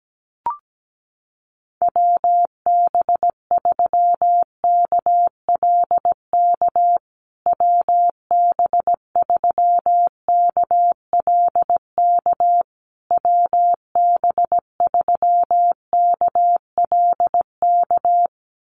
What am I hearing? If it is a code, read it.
Morse code, heard as a keyed tone, about 17 wpm, 700 Hz. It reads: WB3KLK WB3KLK WB3KLK
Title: USA Call Signs Repeated 3x 17wpm